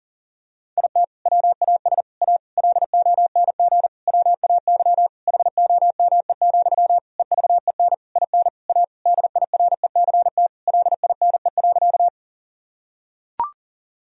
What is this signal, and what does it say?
Morse code, heard as a keyed tone, about 37 wpm, 700 Hz. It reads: IT WAS A LONG WAY HOME, EVEN IN A DIRECT LINE.
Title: Sentences from Top 1000 Words - Review Part 2 - 40wpm